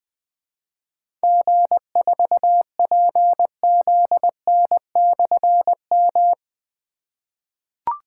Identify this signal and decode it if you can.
Morse code, heard as a keyed tone, about 20 wpm, 700 Hz. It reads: G4PZN/M
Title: Difficult Call Signs 20wpm